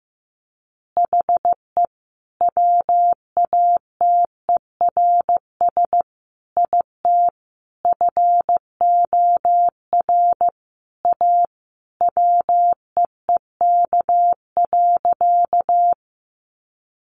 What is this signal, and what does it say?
Morse code, heard as a keyed tone, about 14 wpm, 700 Hz. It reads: HE WATERS IT FOR A WEEK.
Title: Easy Phrases 15wpm